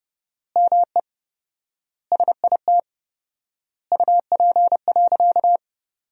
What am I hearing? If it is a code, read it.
Morse code, heard as a keyed tone, about 32 wpm, 700 Hz. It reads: ME SIT UP.
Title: 4x Word-Spacing - Sentences from Top 500 Words - 30wpm